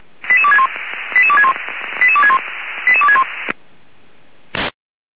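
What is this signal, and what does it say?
Five-tone selective call (ZVEI), heard as quick runs of five tones, tones of about 70 ms. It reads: 88261 88261 88261 88261
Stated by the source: ZVEI five-tone call received by SDR.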